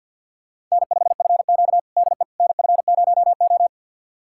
Morse code, heard as a keyed tone, about 47 wpm, 700 Hz. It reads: N5FQ DE NF0O
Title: USA Callsign DE Callsign - 4 Characters - 50wpm